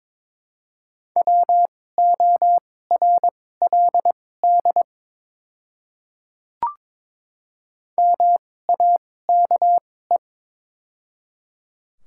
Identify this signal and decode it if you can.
Morse code, heard as a keyed tone, about 22 wpm, spaced as for 16 wpm, 700 Hz. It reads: WORLD MAKE
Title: 2x Character Spacing - Top 200 Words - 22wpm